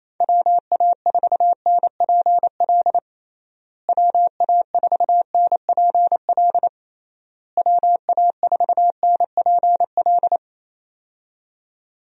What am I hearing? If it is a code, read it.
Morse code, heard as a keyed tone, about 28 wpm, 700 Hz. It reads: WA4NPL WA4NPL WA4NPL
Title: USA Call Signs - 6 Characters - Repeated 3x - 28wpm